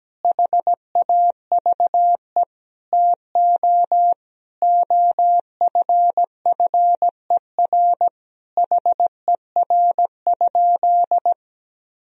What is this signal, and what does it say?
Morse code, heard as a keyed tone, about 17 wpm, 700 Hz. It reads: HAVE TO OFFER HER?
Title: Sentences from Top 1000 Words - Review Part 4 - 17wpm